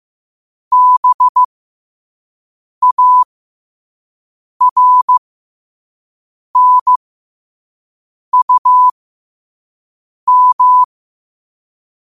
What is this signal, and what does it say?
Morse code, heard as a keyed tone, about 15 wpm, spaced as for 5 wpm, 1000 Hz. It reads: BARNUM
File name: qotdfiles / 5wpm